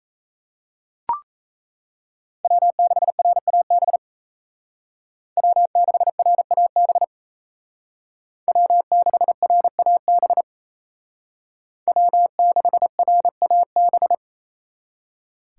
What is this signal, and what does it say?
Morse code, heard as a keyed tone, about 33 wpm, 700 Hz. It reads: W6RAB W6RAB W6RAB W6RAB
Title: Speed Racing - USA Call Signs 28wpm